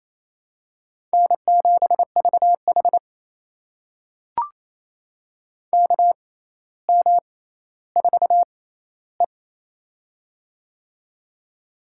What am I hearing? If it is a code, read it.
Morse code, heard as a keyed tone, about 27 wpm, spaced as for 14 wpm, 700 Hz. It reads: N7VH KM4E
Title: USA Call Signs - 4 Characters - 6x Character Spacing - 28wpm